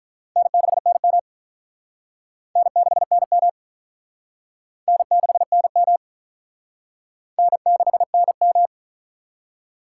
Morse code, heard as a keyed tone, about 45 wpm, 700 Hz. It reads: N6NM N6NM N6NM N6NM